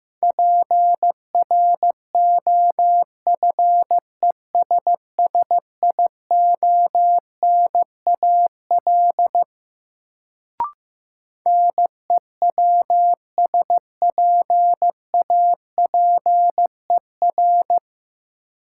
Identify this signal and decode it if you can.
Morse code, heard as a keyed tone, about 15 wpm, 700 Hz. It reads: PROFESSIONAL NEWSPAPER
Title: Sentences from Top 1000 Words - Review Part 4 - New Word Drill - 15wpm